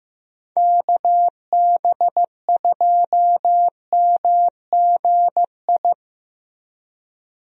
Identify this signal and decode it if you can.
Morse code, heard as a keyed tone, about 15 wpm, 700 Hz. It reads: KB2MGI
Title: USA Call Signs - 6 Characters - Repeated 2x - 15wpm